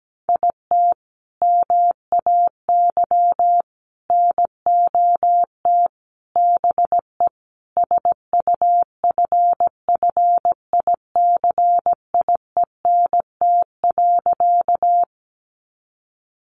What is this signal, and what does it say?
Morse code, heard as a keyed tone, about 17 wpm, 700 Hz. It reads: IT MAY NOT BE SUFFICIENT.
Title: Sentences from Top 1400 Words 17wpm